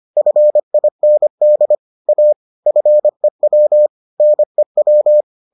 Morse code, heard as a keyed tone, about 25 wpm, 600 Hz. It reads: FIND A FEW NEW